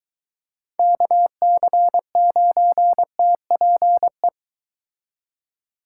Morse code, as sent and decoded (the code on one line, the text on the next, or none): -.- -.-. ----. - .--. .
KC9TPE